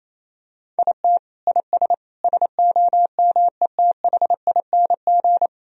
Morse code, heard as a keyed tone, about 28 wpm, 700 Hz. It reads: IT IS SOMETHING
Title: Sentences from Top 1000 Words - Review Part 3 - 28wpm